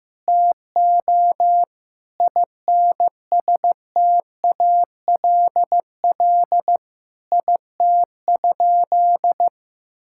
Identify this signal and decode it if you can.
Morse code, heard as a keyed tone, about 15 wpm, 700 Hz. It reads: TO INSTALL IT?
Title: Sentences from Top 1600 Words 15wpm